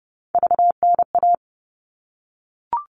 Morse code, heard as a keyed tone, about 30 wpm, 700 Hz. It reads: VNA
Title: All Cut Numbers - 3-Digit 30wpm